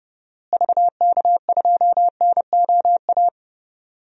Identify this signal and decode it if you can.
Morse code, heard as a keyed tone, about 30 wpm, 700 Hz. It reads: VK2NOA
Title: DXCC-100 Call Signs - 6 Characters - Repeated 2x - 30wpm